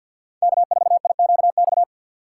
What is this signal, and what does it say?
Morse code, heard as a keyed tone, about 47 wpm, 700 Hz. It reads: K4IQX
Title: USA Call Signs - 5 Characters - 2x Character Spacing - 50wpm